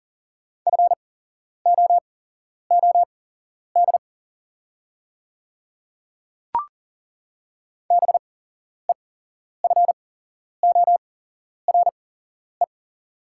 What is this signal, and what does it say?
Morse code, heard as a keyed tone, about 40 wpm, spaced as for 11 wpm, 700 Hz. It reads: FOOD BEFORE
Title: 8x Character Spacing - Top 300 Words - 40wpm